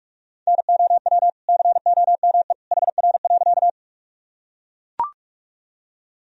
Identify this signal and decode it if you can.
Morse code, heard as a keyed tone, about 44 wpm, 700 Hz. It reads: NOW COME HR.
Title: Sentences from Top 200 Words and Abbreviations 45wpm